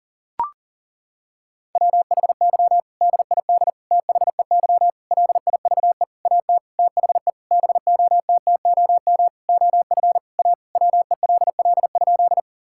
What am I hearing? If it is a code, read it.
Morse code, heard as a keyed tone, about 37 wpm, 700 Hz. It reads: WHY DID THEY LIVE AT THE BOTTOM OF A WELL?
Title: Sentences from Top 1000 Words - Review Part 6 - 40wpm